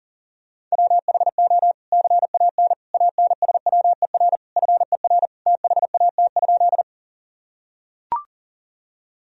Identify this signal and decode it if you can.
Morse code, heard as a keyed tone, about 37 wpm, 700 Hz. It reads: WHO CAN ANSWER FER THAT?
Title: Sentences from Top 300 Words and Abbreviations 40wpm